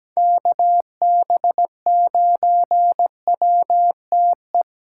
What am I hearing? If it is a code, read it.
Morse code, heard as a keyed tone, about 17 wpm, 700 Hz. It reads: KB9WTE
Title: USA Call Signs - 6 Characters - 4x Character Spacing - 17wpm